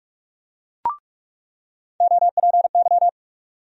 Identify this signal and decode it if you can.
Morse code, heard as a keyed tone, about 45 wpm, 700 Hz. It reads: OPY
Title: USA Call Sign Postfixes - 3 Letters 45wpm